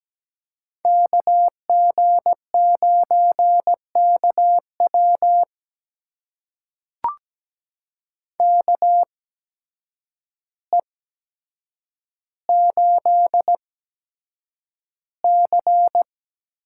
Morse code, heard as a keyed tone, about 17 wpm, spaced as for 8 wpm, 700 Hz. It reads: KG9KW KE8C
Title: USA Call Signs - 8x Character Spacing - 17wpm